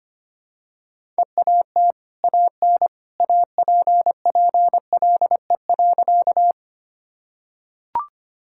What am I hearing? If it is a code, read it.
Morse code, heard as a keyed tone, about 25 wpm, 700 Hz. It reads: EAT AN APPLE.
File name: Easy Sentences with Hints - 25wpm